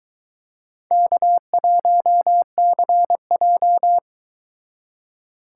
Morse code, heard as a keyed tone, about 23 wpm, 700 Hz. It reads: K1CJ